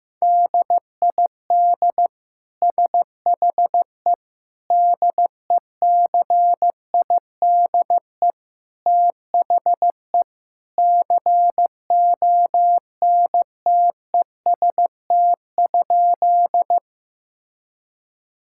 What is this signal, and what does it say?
Morse code, heard as a keyed tone, about 15 wpm, 700 Hz. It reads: DID SHE DECIDE THE CONTEST?
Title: Sentences from Top 1600 Words 15wpm